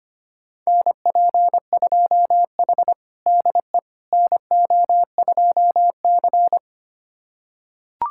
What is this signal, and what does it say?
Morse code, heard as a keyed tone, about 25 wpm, 700 Hz. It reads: NP2H DE NO2C
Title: USA Callsign DE Callsign - 4 Characters - 25wpm